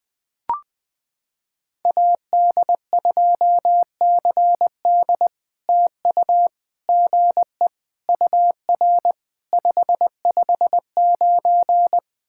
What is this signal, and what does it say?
Morse code, heard as a keyed tone, about 20 wpm, 700 Hz. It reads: AD2CD TU GE UR 559
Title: POTA QSO Practice - 20wpm